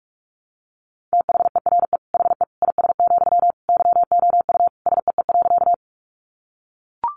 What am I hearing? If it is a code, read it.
Morse code, heard as a keyed tone, about 44 wpm, 700 Hz. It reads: THERE HE IS, YOU SEE.
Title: Sentences from Top 100 Words 45wpm